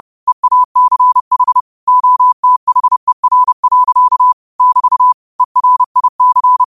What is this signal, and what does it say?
Morse code, heard as a keyed tone, about 30 wpm, 1000 Hz. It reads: EACH OTHER. <BT> ERIC